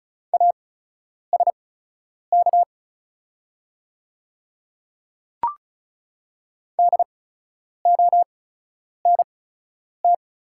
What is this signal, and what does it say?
Morse code, heard as a keyed tone, about 36 wpm, spaced as for 10 wpm, 700 Hz. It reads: ASK DONT
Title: 8x Character Spacing - Top 300 Words - 35wpm